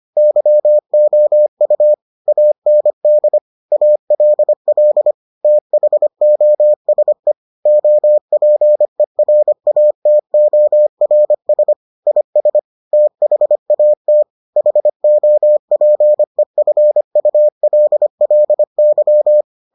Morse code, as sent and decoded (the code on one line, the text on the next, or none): -.-- --- ..- / .- -. -.. / .- .-.. .-.. / - .... --- ... / --- .--. .-. .- - --- .-. ... / .. ... / - .... .- - / .... --- .--. ..-. ..- .-.. .-.. -.--
YOU AND ALL THOSE OPERATORS IS THAT HOPEFULLY